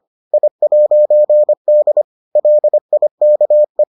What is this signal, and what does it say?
Morse code, heard as a keyed tone, about 25 wpm, 600 Hz. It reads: I'D LIKE